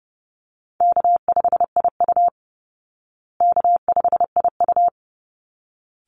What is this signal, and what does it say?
Morse code, heard as a keyed tone, about 30 wpm, 700 Hz. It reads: K5IU K5IU
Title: Easy Call Signs Repeated 2x - 30wpm